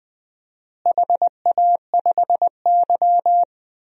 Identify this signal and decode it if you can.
Morse code, heard as a keyed tone, about 21 wpm, 700 Hz. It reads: HA5Y
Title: DXCC-100 Call Signs - 4 Characters - Repeated 3x - 20wpm